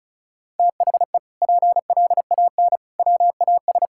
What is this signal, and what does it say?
Morse code, heard as a keyed tone, about 34 wpm, 700 Hz. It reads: THE PLAN WAS
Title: Sentences from Top 1000 Words - Review Part 6 - 35wpm